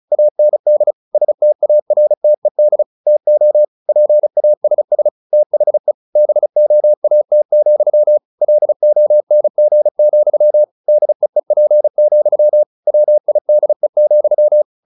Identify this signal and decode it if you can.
Morse code, heard as a keyed tone, about 34 wpm, 600 Hz. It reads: AND STARTED TO PASS THE BOAT, LONG, DEEP, WIDE,
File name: the old man and the sea 2 35wpm